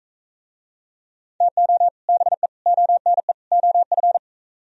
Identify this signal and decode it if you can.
Morse code, heard as a keyed tone, about 45 wpm, 700 Hz. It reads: TO BE ONE OF